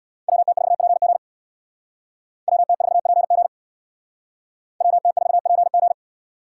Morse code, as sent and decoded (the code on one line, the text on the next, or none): .-- - ....- .--. --. / .-- - ....- .--. --. / .-- - ....- .--. --.
WT4PG WT4PG WT4PG